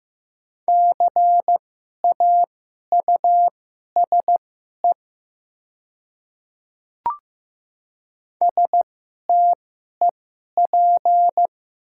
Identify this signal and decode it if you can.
Morse code, heard as a keyed tone, about 15 wpm, spaced as for 11 wpm, 700 Hz. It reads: CAUSE STEP
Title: 2x Character Spacing - Top 500 Words - 15wpm